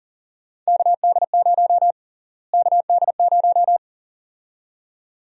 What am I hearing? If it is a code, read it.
Morse code, heard as a keyed tone, about 37 wpm, 700 Hz. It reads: KD0 KD0
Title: USA Call Sign Prefixes - Repeated 2x - 40wpm